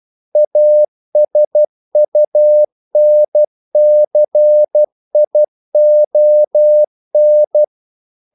Morse code, heard as a keyed tone, about 12 wpm, 600 Hz. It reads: ASUNCION
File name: CW / Practice / NationalCapitals12